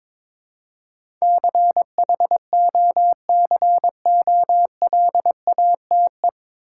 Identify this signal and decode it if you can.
Morse code, heard as a keyed tone, about 22 wpm, 700 Hz. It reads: CHOCOLATE
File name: 9 Letter Words - First 3 Letters Given - 22wpm